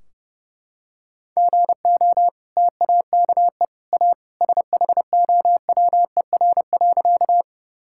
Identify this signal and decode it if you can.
Morse code, heard as a keyed tone, about 30 wpm, 700 Hz. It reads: GO TAKE A SHOWER.
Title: Sentences from Top 1600 Words 30wpm